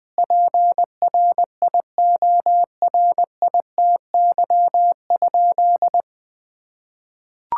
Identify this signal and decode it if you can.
Morse code, heard as a keyed tone, about 21 wpm, 700 Hz. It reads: PRIORITY?
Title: Sentences from Top 1600 Words 20wpm